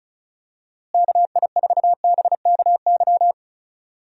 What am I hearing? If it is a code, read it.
Morse code, heard as a keyed tone, about 36 wpm, 700 Hz. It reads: KI4BKY
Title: USA Call Signs - 6 Characters - Repeated 2x - 35wpm